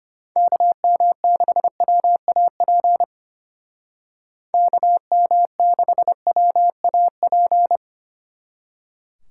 Morse code, heard as a keyed tone, about 30 wpm, 700 Hz. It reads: KM6WAP KM6WAP
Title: Speed Racing - USA Call Signs 25wpm